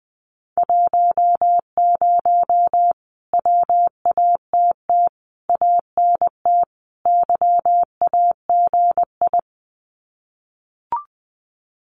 Morse code, heard as a keyed tone, about 20 wpm, 700 Hz. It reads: 10 WATT ANT YAGI
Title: Common QSO Phrases Encore 20wpm